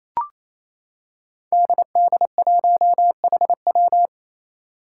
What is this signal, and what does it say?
Morse code, heard as a keyed tone, about 28 wpm, 700 Hz. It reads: DD1HW
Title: Five Challenging Letter-Number Combinations 28wpm